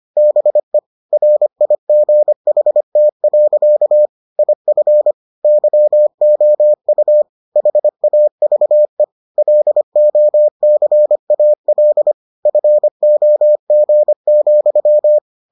Morse code, heard as a keyed tone, about 25 wpm, 600 Hz. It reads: BE RIGHT. IF YOU HAVE LOCAL FOG,